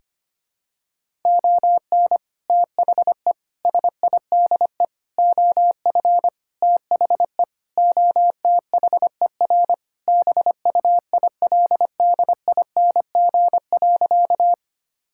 Morse code, heard as a keyed tone, about 25 wpm, 700 Hz. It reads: ON THE SIDE OF THE OTHER BUILDING.
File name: practice / 25wpm / Sentences with 3-Word Phrases - 25wpm